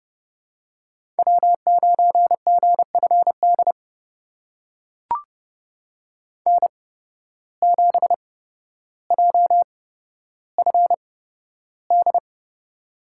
Morse code, heard as a keyed tone, about 32 wpm, spaced as for 13 wpm, 700 Hz. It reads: W9GFD N7JFD
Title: USA Call Signs - 8x Character Spacing - 30wpm